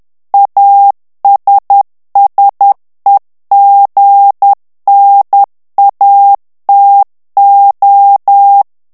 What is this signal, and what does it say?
Morse code, heard as a keyed tone, about 10 wpm, 800 Hz. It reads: ASSEGNATO
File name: ITALIEN texte clair 2 vitesse 10 mots minute